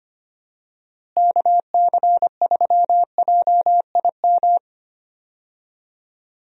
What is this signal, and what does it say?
Morse code, heard as a keyed tone, about 25 wpm, 700 Hz. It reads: KC3JIM
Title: USA Call Signs - 6 Characters - Repeated 3x - 25wpm